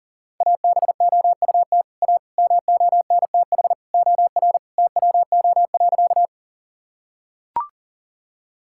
Morse code, heard as a keyed tone, about 40 wpm, 700 Hz. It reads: ABOUT A MONTH OR TWO.
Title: Sentences from Top 1000 Words - Review Part 5 - 40wpm